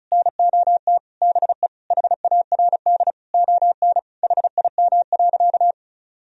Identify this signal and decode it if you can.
Morse code, heard as a keyed tone, about 36 wpm, 700 Hz. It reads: NOT BE HARD ON HIM.
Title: Sentences from Top 1000 Words - Review Part 1 - 35wpm